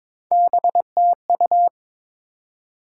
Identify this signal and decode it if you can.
Morse code, heard as a keyed tone, about 21 wpm, 700 Hz. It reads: BTU
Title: Top 90 QSO Elements 22wpm